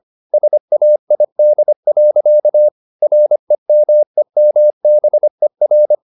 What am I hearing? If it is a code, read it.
Morse code, heard as a keyed tone, about 25 wpm, 600 Hz. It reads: SAID. REMEMBER
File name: the old man and the sea part 1 25wpm